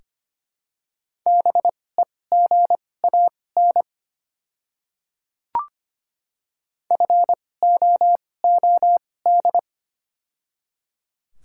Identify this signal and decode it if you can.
Morse code, heard as a keyed tone, about 25 wpm, spaced as for 18 wpm, 700 Hz. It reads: BEGAN FOOD
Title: 2x Character Spacing - Top 300 Words - 25wpm